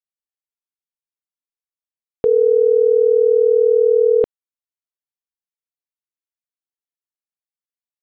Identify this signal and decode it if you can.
Ringback tone (North America), 440 + 480 Hz.